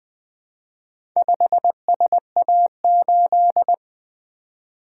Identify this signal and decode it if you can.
Morse code, heard as a keyed tone, about 20 wpm, 700 Hz. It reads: 5SA8
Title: Four Letter-Number Combinations - 20wpm